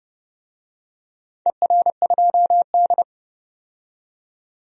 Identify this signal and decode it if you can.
Morse code, heard as a keyed tone, about 29 wpm, 700 Hz. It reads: ER2D
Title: DXCC-100 Call Signs - 4 Characters - 30wpm